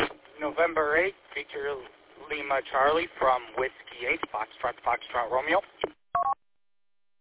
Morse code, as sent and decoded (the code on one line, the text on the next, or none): none